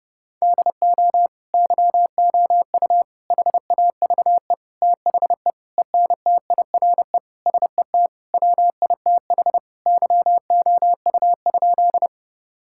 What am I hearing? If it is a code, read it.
Morse code, heard as a keyed tone, about 30 wpm, 700 Hz. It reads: DO YOU HAVE THE ENTIRE SET WITH YOU?
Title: Sentences from Top 1000 Words - Review Part 5 - 30wpm